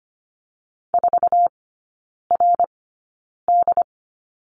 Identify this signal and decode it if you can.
Morse code, heard as a keyed tone, about 25 wpm, spaced as for 9 wpm, 690 Hz. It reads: 4RD
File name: Lesson 15e - Number 5 - Three Characters 25wpm